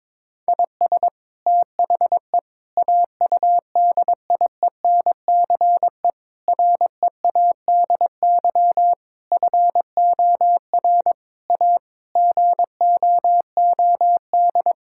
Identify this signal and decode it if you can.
Morse code, heard as a keyed tone, about 21 wpm, 700 Hz. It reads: IS THE AUDIENCE READY FOR A GOOD
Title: Sentences from Top 1000 Words - Review Part 5 - 22wpm